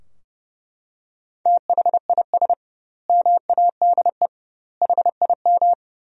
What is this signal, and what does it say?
Morse code, heard as a keyed tone, about 30 wpm, 700 Hz. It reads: THIS MADE HIM
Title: 2x Word-Spacing - Sentences from Top 500 Words - 30wpm